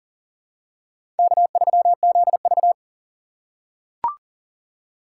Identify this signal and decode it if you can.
Morse code, heard as a keyed tone, about 40 wpm, 700 Hz. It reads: K3ZV
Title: USA Call Signs - 4 Characters - 6x Character Spacing - 40wpm